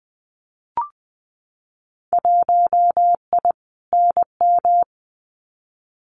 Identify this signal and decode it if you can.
Morse code, heard as a keyed tone, about 20 wpm, 700 Hz. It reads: 1I NM
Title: Winter Field Day Exchanges Encore 20wpm